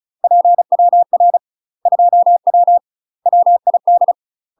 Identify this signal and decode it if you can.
Morse code, heard as a keyed tone, about 36 wpm, 700 Hz. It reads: PWR 2W WID
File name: UK MASTER QSOs TRAINING / 35WPM / UK MASTER QSOs 0031